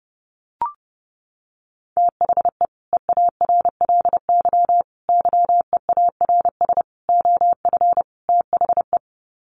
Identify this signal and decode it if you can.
Morse code, heard as a keyed tone, about 30 wpm, 700 Hz. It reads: THE EARLY YEARS OF THE
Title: Sets of 5 Words Encore 30wpm